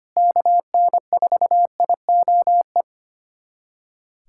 Morse code, heard as a keyed tone, about 25 wpm, 700 Hz. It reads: KN4IOE
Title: USA Callsign DE Callsign 25wpm